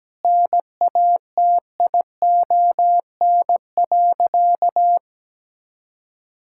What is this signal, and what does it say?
Morse code, heard as a keyed tone, about 17 wpm, 700 Hz. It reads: NATION.
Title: Sentences from Top 1000 Words - Review Part 6 - 17wpm